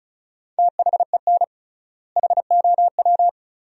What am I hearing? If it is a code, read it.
Morse code, heard as a keyed tone, about 36 wpm, 700 Hz. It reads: THEN HOW
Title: 3x Word-Spacing - Sentences from Top 100 Words - 35wpm